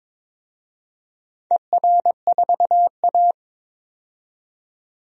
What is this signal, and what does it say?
Morse code, heard as a keyed tone, about 22 wpm, 700 Hz. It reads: ER4A
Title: DXCC-100 Call Signs - 4 Characters - 22wpm